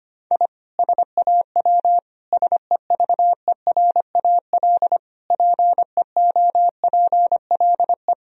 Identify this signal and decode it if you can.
Morse code, heard as a keyed tone, about 25 wpm, 700 Hz. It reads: I SAW SEVERAL PEOPLE
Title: Sentences from Top 1000 Words - Review Part 2 - 25wpm